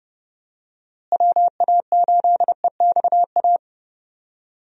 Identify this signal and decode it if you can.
Morse code, heard as a keyed tone, about 30 wpm, 700 Hz. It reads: WA8EXA
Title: USA Call Signs - 6 Characters - Repeated 2x - 30wpm